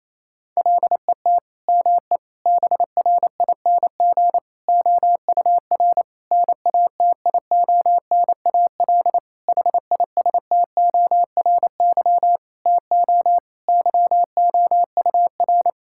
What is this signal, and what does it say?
Morse code, heard as a keyed tone, about 27 wpm, 690 Hz. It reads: LET ME BRING OUR NATIONAL HISTORY TO YOUR
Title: Sentences from Top 1000 Words - Review Part 6 - 28wpm